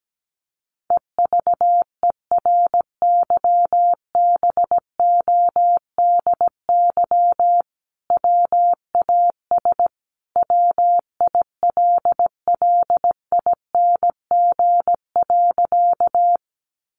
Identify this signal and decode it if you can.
Morse code, heard as a keyed tone, about 17 wpm, 700 Hz. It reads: EVERYBODY WAS WILLING.
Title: Sentences from Top 1400 Words 17wpm